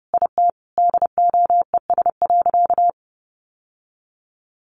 Morse code, heard as a keyed tone, about 30 wpm, 700 Hz. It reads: IT DOES.
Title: Sentences from Top 900 Words 30wpm